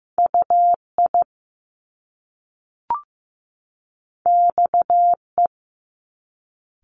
Morse code, heard as a keyed tone, about 15 wpm, 700 Hz. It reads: UI XE
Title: Top Two Letter Combinations in Words 15wpm